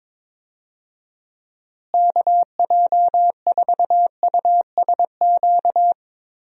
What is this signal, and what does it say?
Morse code, heard as a keyed tone, about 22 wpm, 700 Hz. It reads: KJ4USQ
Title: USA Call Signs - 6 Characters - Repeated 2x - 22wpm